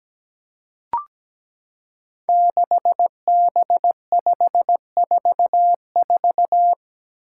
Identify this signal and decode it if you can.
Morse code, heard as a keyed tone, about 17 wpm, 700 Hz. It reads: 6B544
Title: Five Challenging Letter-Number Combinations 17wpm